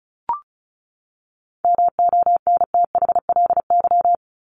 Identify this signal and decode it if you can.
Morse code, heard as a keyed tone, about 34 wpm, 700 Hz. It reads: MONTHLY
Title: LY Suffix - Top 100 Words with Base Words 35wpm